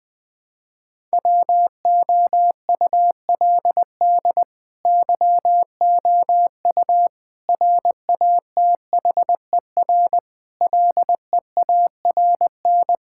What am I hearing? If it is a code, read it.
Morse code, heard as a keyed tone, about 21 wpm, 700 Hz. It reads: WOULD YOU RATHER LEARN
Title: Sentences from Top 1000 Words - Review Part 6 - 20wpm